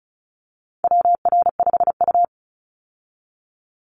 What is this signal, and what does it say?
Morse code, heard as a keyed tone, about 34 wpm, 700 Hz. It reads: WR5U
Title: Call Signs Repeated 2x - 35wpm